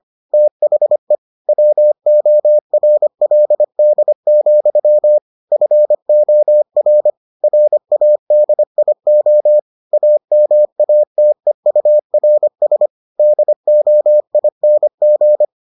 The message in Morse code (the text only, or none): THE WORLD, FOR RADIO AMATEURS DOING